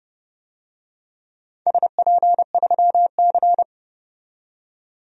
Morse code, heard as a keyed tone, about 32 wpm, 700 Hz. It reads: SP3C
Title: DXCC-100 Call Signs - 4 Characters - 30wpm